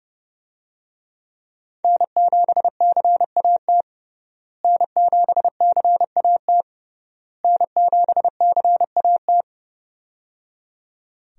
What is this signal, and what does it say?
Morse code, heard as a keyed tone, about 29 wpm, 700 Hz. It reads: N7CAT N7CAT N7CAT
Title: USA Call Signs - 5 Characters - Repeated 3x - 30wpm